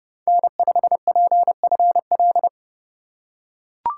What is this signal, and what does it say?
Morse code, heard as a keyed tone, about 31 wpm, 700 Hz. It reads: N5PFL